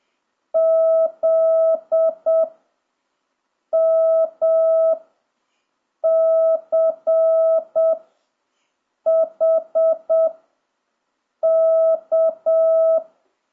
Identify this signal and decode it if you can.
Morse code, heard as a keyed tone, about 7 wpm, 1300 Hz. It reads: Z M C H K
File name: jh8cbh / 2026cw